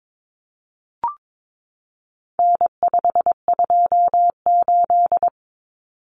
Morse code, heard as a keyed tone, about 22 wpm, 700 Hz. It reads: N528
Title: Common Cut Numbers - 4-Digit 22wpm